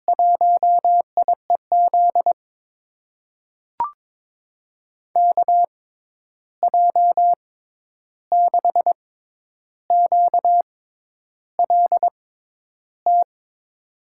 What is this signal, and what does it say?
Morse code, heard as a keyed tone, about 21 wpm, spaced as for 10 wpm, 700 Hz. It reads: 1IEZ KJ6QLT
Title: USA Call Signs - 6 Characters - 6x Character Spacing - 22wpm